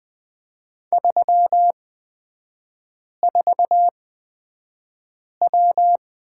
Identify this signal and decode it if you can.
Morse code, heard as a keyed tone, about 20 wpm, spaced as for 5 wpm, 700 Hz. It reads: 34W
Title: 236 Lesson 38e - Letter q - Three Characters 20wpm